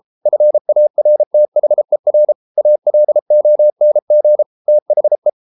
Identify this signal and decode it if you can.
Morse code, heard as a keyed tone, about 33 wpm, 600 Hz. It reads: FARTHER ALONG THE